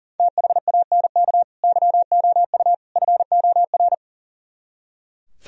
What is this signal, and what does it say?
Morse code, heard as a keyed tone, about 38 wpm, 700 Hz. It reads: THANK YOU FOR
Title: Speed-Racing - Sets of 3 Words 40wpm